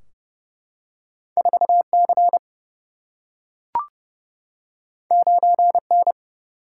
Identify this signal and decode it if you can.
Morse code, heard as a keyed tone, about 30 wpm, 700 Hz. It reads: 4 C 9 N
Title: Single-Letter Single-Number 30wpm